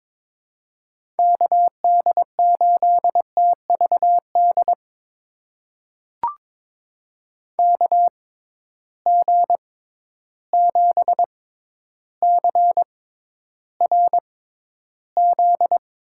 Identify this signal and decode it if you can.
Morse code, heard as a keyed tone, about 22 wpm, spaced as for 11 wpm, 700 Hz. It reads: KD8TVD KG7CRZ
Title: USA Call Signs - 6 Characters - 6x Character Spacing - 22wpm